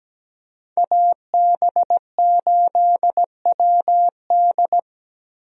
Morse code, heard as a keyed tone, about 17 wpm, 700 Hz. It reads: AB8WD
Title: USA Call Signs - 5 Characters - 17wpm